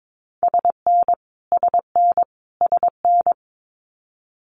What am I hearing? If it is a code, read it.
Morse code, heard as a keyed tone, about 22 wpm, 700 Hz. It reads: SN SN SN
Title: Multi-letter Phonemes Repeated 3x 22wpm